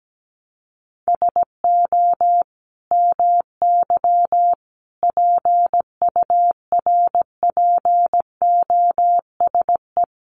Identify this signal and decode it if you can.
Morse code, heard as a keyed tone, about 17 wpm, 700 Hz. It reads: SO MY PURPOSE
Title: Sentences from Top 1100 Words 17wpm